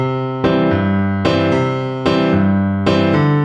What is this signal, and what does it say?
There is no code.